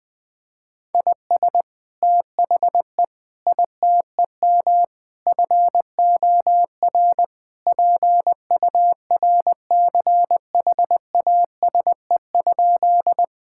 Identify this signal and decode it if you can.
Morse code, heard as a keyed tone, about 20 wpm, 700 Hz. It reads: IS THE ITEM FOR PURCHASE?
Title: Sentences from Top 1000 Words - Review Part 6 - 20wpm